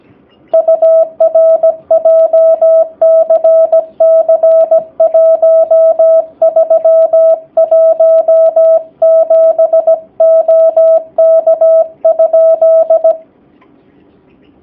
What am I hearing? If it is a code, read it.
Morse code, heard as a keyed tone, about 17 wpm, 650 Hz. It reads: UR JCC 1317 OK?
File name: jh8cbh / 2019cw / sp16- w